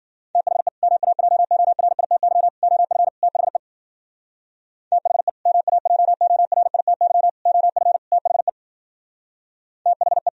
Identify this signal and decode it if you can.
Morse code, heard as a keyed tone, about 51 wpm, 700 Hz. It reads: THE MAJORITY OF THE THE MAJORITY OF THE THE